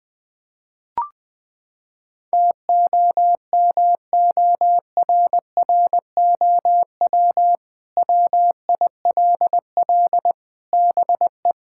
Morse code, heard as a keyed tone, about 20 wpm, 700 Hz. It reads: TOMORROW WILL BE
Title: Sentences with 2-Word Phrases - 20wpm